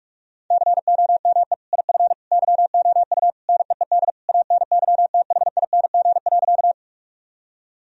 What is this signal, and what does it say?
Morse code, heard as a keyed tone, about 46 wpm, 700 Hz. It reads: COME IF YOU NEED ANYTHING.
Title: Sentences from Top 1000 Words - Review Part 1 - 45wpm